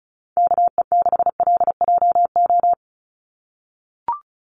Morse code, heard as a keyed tone, about 34 wpm, 700 Hz. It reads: KE6LJO
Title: USA Call Signs 35wpm